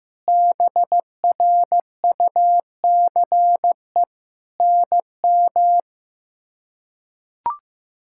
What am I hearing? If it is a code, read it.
Morse code, heard as a keyed tone, about 16 wpm, 700 Hz. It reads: BRUCE NM
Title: K1USN SST QSO Practice - 15wpm